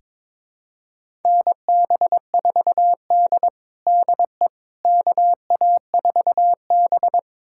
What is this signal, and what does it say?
Morse code, heard as a keyed tone, about 21 wpm, 700 Hz. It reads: NB4D DE KA4B
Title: USA Callsign DE Callsign - 4 Characters - 22wpm